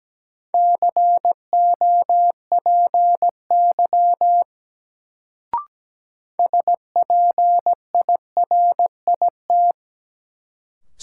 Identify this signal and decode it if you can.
Morse code, heard as a keyed tone, about 17 wpm, 700 Hz. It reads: COPY SPIRIT
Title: Sentences from Top 1400 Words - New Word Drill - 17wpm